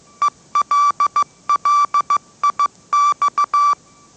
Morse code, heard as a keyed tone, about 15 wpm, 1160 Hz. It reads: ELLIX
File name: mrx / mrx065 16wpm